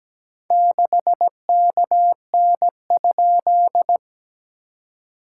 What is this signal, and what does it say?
Morse code, heard as a keyed tone, about 17 wpm, 700 Hz. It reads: 6KN?
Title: SOTA QSO Practice 17wpm